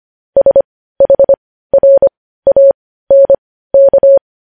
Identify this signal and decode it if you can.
Morse code, heard as a keyed tone, about 25 wpm, spaced as for 15 wpm, 550 Hz. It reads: SHRANK
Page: news-headlines-in-morse-code-at-15-wpm